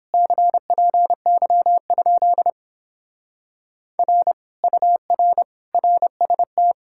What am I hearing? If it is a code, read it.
Morse code, heard as a keyed tone, about 31 wpm, 700 Hz. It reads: CPY? R UR RST